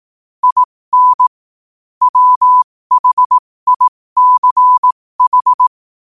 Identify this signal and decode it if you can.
Morse code, heard as a keyed tone, about 18 wpm, spaced as for 15 wpm, 1000 Hz. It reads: IN WHICH
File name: qotdfiles / 15wpm